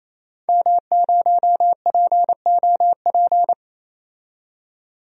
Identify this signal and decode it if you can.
Morse code, heard as a keyed tone, about 28 wpm, 700 Hz. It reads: M0POP
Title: DXCC-100 Call Signs - 5 Characters - 28wpm